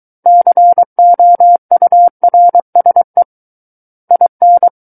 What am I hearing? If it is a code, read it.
Morse code, heard as a keyed tone, about 23 wpm, 700 Hz. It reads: COURSE IN
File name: Book One / 14-Chapter 14 - 23 WPM